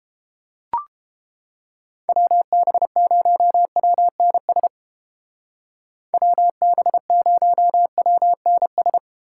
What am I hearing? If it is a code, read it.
Morse code, heard as a keyed tone, about 29 wpm, 700 Hz. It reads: WB0WNS WB0WNS